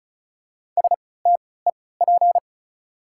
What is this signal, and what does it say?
Morse code, heard as a keyed tone, about 34 wpm, spaced as for 20 wpm, 700 Hz. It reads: STEP
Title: 3x Character Spacing - Top 500 Words - 35wpm